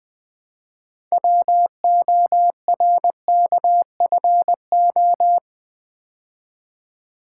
Morse code, heard as a keyed tone, about 20 wpm, 700 Hz. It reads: WORKFO
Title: Build A Word - 3 Letters At A Time - 9-Letter Words - 20wpm